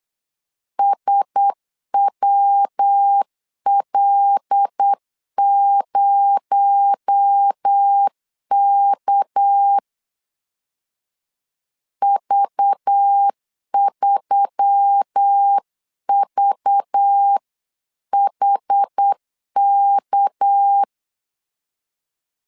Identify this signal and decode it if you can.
Morse code, heard as a keyed tone, about 8 wpm, 800 Hz. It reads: SWL0K V3VHK
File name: 8wpm